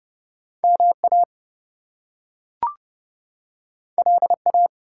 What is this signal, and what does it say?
Morse code, heard as a keyed tone, about 30 wpm, 700 Hz. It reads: MA LA
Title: US States and Canadian Provinces 30wpm